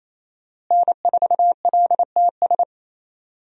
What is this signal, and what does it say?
Morse code, heard as a keyed tone, about 28 wpm, 700 Hz. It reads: N4LTS
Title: USA Call Signs - 5 Characters - Repeated 2x - 28wpm